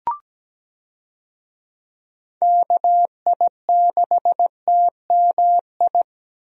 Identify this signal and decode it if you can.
Morse code, heard as a keyed tone, about 17 wpm, 700 Hz. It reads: KI6TMI
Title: USA Call Signs - 6 Characters - Repeated 3x - 17wpm